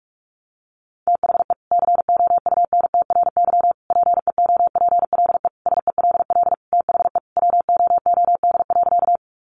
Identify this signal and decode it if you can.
Morse code, heard as a keyed tone, about 44 wpm, 700 Hz. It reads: THE COUNTRY PEOPLE SELL THE WOOD.
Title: Sentences from Top 1100 Words 45wpm